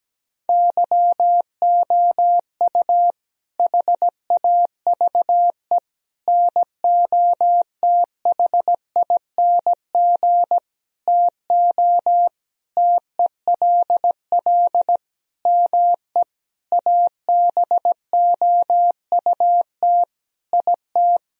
Morse code, heard as a keyed tone, about 17 wpm, 700 Hz. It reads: YOU HAVE NOTHING TO TELL ME ABOUT IT
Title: Sentences from Top 1000 Words - Review Part 4 - 17wpm